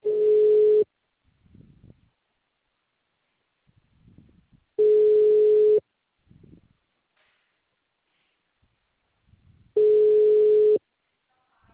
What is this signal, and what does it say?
Ringback tone (Europe, 425 Hz).